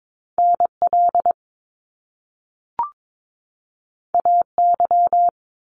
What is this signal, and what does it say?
Morse code, heard as a keyed tone, about 22 wpm, 700 Hz. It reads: NL AY
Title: Top Two Letter Combinations in Words 22wpm